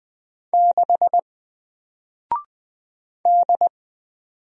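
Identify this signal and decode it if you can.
Morse code, heard as a keyed tone, about 20 wpm, 700 Hz. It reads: 6 D
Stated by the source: Single letter and number Morse drill at 20 wpm, from N4LCD.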